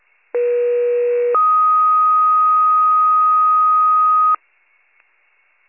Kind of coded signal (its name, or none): two-tone page